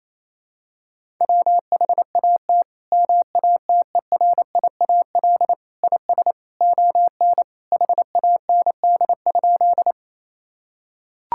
Morse code, heard as a keyed tone, about 28 wpm, 700 Hz. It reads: WHAT MATERIAL IS ON HAND?
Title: Sentences from Top 1000 Words - Review Part 5 - 28wpm